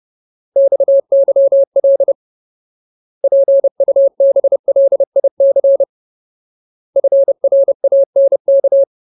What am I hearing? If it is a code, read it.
Morse code, heard as a keyed tone, about 30 wpm, 550 Hz. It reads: XYL PUBLIC FRANK
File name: Ultimate Word List / 30WPM Ultimate